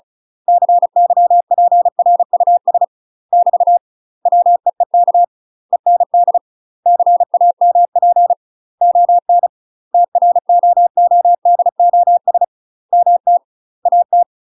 Morse code, heard as a keyed tone, about 34 wpm, 700 Hz. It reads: CYPRUS <BT> WEEK END CAMP ON TROODOS MT AT
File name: UK MASTER QSOs TRAINING/35WPM